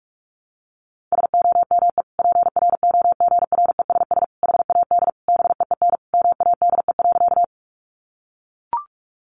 Morse code, heard as a keyed tone, about 44 wpm, 700 Hz. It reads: SOME PROGRESS HAD BEEN MADE.
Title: Sentences from Top 1400 Words 45wpm